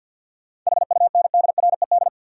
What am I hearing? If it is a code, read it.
Morse code, heard as a keyed tone, about 51 wpm, 700 Hz. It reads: HUNDRED
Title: Sentences from Top 400 Words - New Word Drill - 50wpm